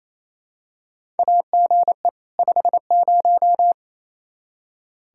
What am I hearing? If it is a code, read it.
Morse code, heard as a keyed tone, about 28 wpm, 700 Hz. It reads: AGE 50
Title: Practice QSO - Exchange 3 - More Info - 28wpm